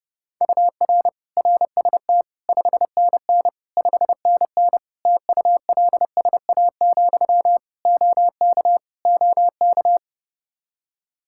Morse code, heard as a keyed tone, about 29 wpm, 700 Hz. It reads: UR RST 5NN 5NN TULSA, OK OK
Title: Practice QSO - Exchange 1 - Beginning - 30wpm